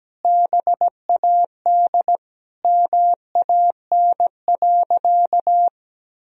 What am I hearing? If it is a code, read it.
Morse code, heard as a keyed tone, about 17 wpm, 700 Hz. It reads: BAD MAN.
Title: Sentences from Top 1000 Words - Review Part 1 - 17wpm